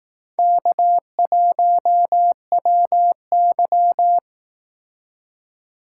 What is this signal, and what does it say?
Morse code, heard as a keyed tone, about 18 wpm, 700 Hz. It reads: K1WY
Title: Speed Racing - USA 1x2 Call Signs 15wpm